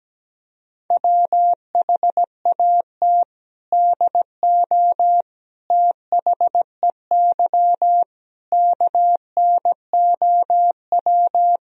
Morse code, heard as a keyed tone, about 17 wpm, 700 Hz. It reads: WHAT DO THEY KNOW
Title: Sentences from Top 1000 Words - Review Part 3 - 17wpm